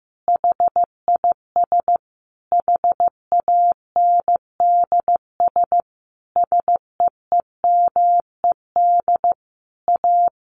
Morse code, heard as a keyed tone, about 15 wpm, 700 Hz. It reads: HIS HANDS SEEMED A
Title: A Sound Of Thunder 15wpm